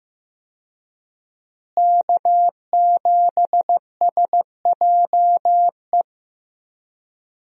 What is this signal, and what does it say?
Morse code, heard as a keyed tone, about 15 wpm, 700 Hz. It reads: K7SJE
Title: USA Call Signs - 5 Characters - Repeated 2x - 15wpm